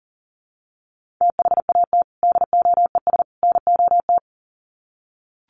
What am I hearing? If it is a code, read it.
Morse code, heard as a keyed tone, about 40 wpm, 700 Hz. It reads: THAT DOES NOT
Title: Sets of 3 Words 40wpm